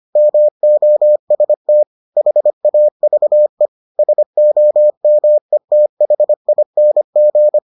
Morse code, heard as a keyed tone, about 25 wpm, 600 Hz. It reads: MOST HAVE SOMETHING